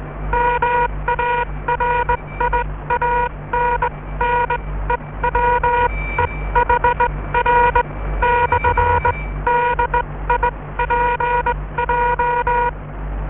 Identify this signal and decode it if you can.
Morse code, heard as a keyed tone, about 19 wpm, spaced as for 16 wpm, 960 Hz, with noise. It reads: MARIANNE WEHR/DIPJ